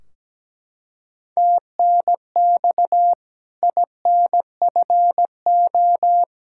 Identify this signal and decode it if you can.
Morse code, heard as a keyed tone, about 17 wpm, 700 Hz. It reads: TNX INFO
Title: K1USN SST QSO Practice 17wpm